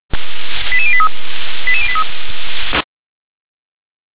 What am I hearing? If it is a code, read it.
Five-tone selective call (ZVEI), heard as quick runs of five tones, tones of about 70 ms. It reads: 88083 88083